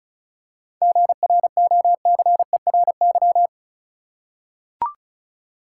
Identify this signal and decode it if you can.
Morse code, heard as a keyed tone, about 36 wpm, 700 Hz. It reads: GROCERY
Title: Seven Letter Words - 35wpm